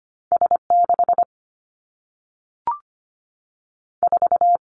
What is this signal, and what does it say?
Morse code, heard as a keyed tone, about 25 wpm, 700 Hz. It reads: S 6 4
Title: Two Challenging Letter-Number Combinations 25wpm